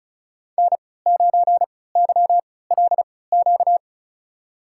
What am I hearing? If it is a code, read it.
Morse code, heard as a keyed tone, about 36 wpm, spaced as for 20 wpm, 700 Hz. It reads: N9YLQ
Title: USA Call Signs - 3x Character Spacing - 35wpm